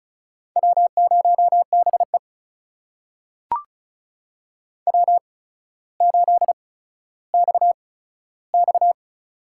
Morse code, heard as a keyed tone, about 36 wpm, spaced as for 15 wpm, 700 Hz. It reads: W0BE W8XX
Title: USA Call Signs - 4 Characters - 8x Character Spacing - 35wpm